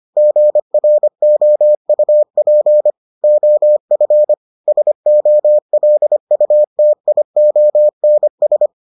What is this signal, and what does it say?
Morse code, heard as a keyed tone, about 25 wpm, 600 Hz. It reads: GROUP OF SOLUTIONS